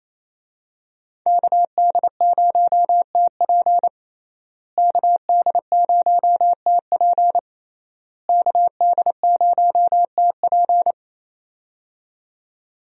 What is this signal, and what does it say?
Morse code, heard as a keyed tone, about 28 wpm, 700 Hz. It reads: KD0TP KD0TP KD0TP
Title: USA Call Signs - 5 Characters - Repeated 3x - 28wpm